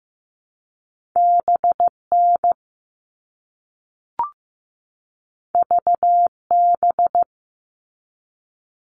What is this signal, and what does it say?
Morse code, heard as a keyed tone, about 15 wpm, 700 Hz. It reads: BN VB
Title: All Cut Numbers - 2-Digit 15wpm